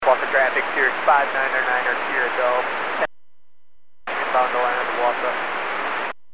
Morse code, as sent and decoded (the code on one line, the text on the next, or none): none